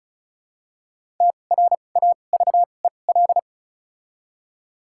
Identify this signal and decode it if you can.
Morse code, heard as a keyed tone, about 36 wpm, spaced as for 25 wpm, 700 Hz. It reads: TRAVEL
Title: 2x Character Spacing - Top 500 Words - 35wpm